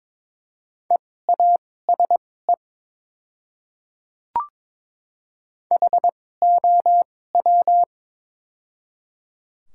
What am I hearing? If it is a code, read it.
Morse code, heard as a keyed tone, about 22 wpm, spaced as for 16 wpm, 700 Hz. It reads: EASE HOW
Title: 2x Character Spacing - Top 300 Words - 22wpm